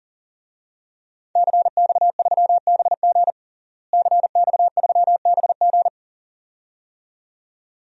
Morse code, heard as a keyed tone, about 40 wpm, 700 Hz. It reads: CX3BG CX3BG
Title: DXCC-100 Call Signs - 5 Characters - Repeated 2x - 40wpm